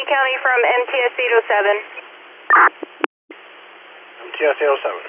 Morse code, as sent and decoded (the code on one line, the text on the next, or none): none